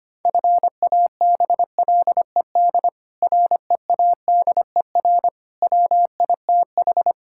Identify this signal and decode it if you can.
Morse code, heard as a keyed tone, about 25 wpm, 700 Hz. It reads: FABLED READER WITH
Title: Pangram Challenge v2 - 25wpm